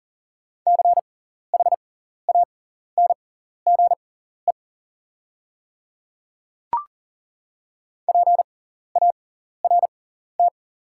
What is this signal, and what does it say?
Morse code, heard as a keyed tone, about 40 wpm, spaced as for 14 wpm, 700 Hz. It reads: CHANGE PART
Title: 6x Character Spacing - Top 300 Words - 40wpm